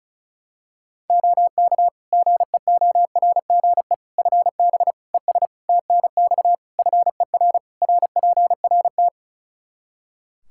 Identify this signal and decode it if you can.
Morse code, heard as a keyed tone, about 34 wpm, 700 Hz. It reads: OK GEORGE FB ES TNX FER RPRT
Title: K1USN SST QSO Practice 35wpm